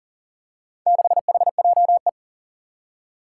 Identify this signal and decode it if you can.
Morse code, heard as a keyed tone, about 40 wpm, 700 Hz. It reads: 6HJE
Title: Four Challenging Letter-Number Combinations 40wpm